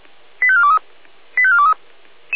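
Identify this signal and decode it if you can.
Five-tone selective call (ZVEI), heard as quick runs of five tones, tones of about 80 ms. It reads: 85423 85423
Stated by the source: ZVEI five-tone selective call received by SDR.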